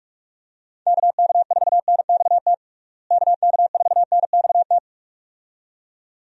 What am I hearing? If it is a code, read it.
Morse code, heard as a keyed tone, about 41 wpm, 700 Hz. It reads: KK4NXT KK4NXT
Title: USA Call Signs - 6 Characters - Repeated 2x - 45wpm